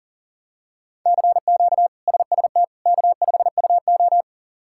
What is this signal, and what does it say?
Morse code, heard as a keyed tone, about 40 wpm, 700 Hz. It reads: CQ SST K5UO
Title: K1USN SST QSO Practice - 40wpm